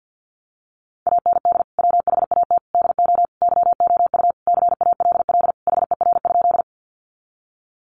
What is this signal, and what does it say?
Morse code, heard as a keyed tone, about 50 wpm, 700 Hz. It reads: AND WHAT DO YOU CALL HER?
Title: Sentences from Top 100 Words 50wpm